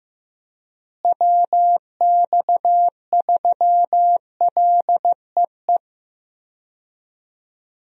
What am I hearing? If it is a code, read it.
Morse code, heard as a keyed tone, about 15 wpm, 700 Hz. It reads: WX3LEE
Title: USA Call Signs - 6 Characters - Repeated 2x - 15wpm